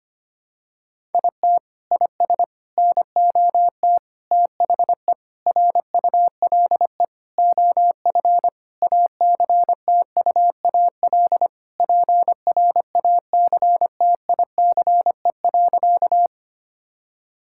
Morse code, heard as a keyed tone, about 25 wpm, 700 Hz. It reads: IT IS NOT THE RULE OF ACTUAL PRACTICE.
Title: Sentences from Top 1000 Words - Review Part 5 - 25wpm